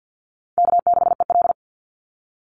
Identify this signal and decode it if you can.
Morse code, heard as a keyed tone, about 50 wpm, 700 Hz. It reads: K6EL
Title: NA SOTA Chaser Callsigns 50wpm